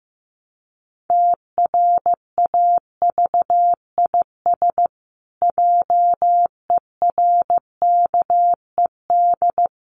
Morse code, heard as a keyed tone, about 15 wpm, 700 Hz. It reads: TRAVIS JERKED
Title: A Sound Of Thunder 15wpm